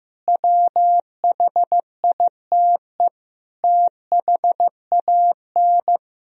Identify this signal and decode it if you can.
Morse code, heard as a keyed tone, about 15 wpm, 700 Hz. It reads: WHITE THAN
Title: Sentences from Top 1000 Words - Review Part 1 - 15wpm